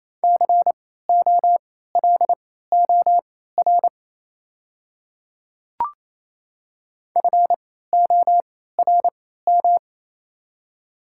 Morse code, heard as a keyed tone, about 28 wpm, spaced as for 16 wpm, 700 Hz. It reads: COLOR FORM
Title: 3x Character Spacing - Top 300 Words - 28wpm